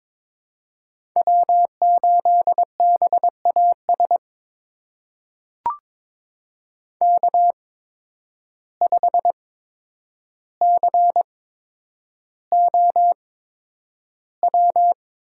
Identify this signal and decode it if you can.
Morse code, heard as a keyed tone, about 22 wpm, spaced as for 9 wpm, 700 Hz. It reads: W8BAS K5COW
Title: USA Call Signs - 5 Characters - 8x Character Spacing - 22wpm